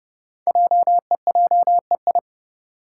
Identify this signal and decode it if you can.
Morse code, heard as a keyed tone, about 30 wpm, 700 Hz. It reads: JEJEI
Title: Code Groups - 5 Letter:Numbers - 30wpm